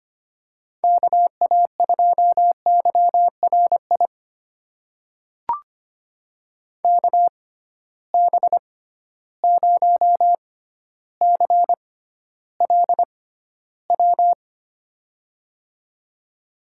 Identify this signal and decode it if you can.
Morse code, heard as a keyed tone, about 25 wpm, spaced as for 13 wpm, 700 Hz. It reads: KA2YRI KB0CLW
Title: USA Call Signs - 6 Characters - 6x Character Spacing - 25wpm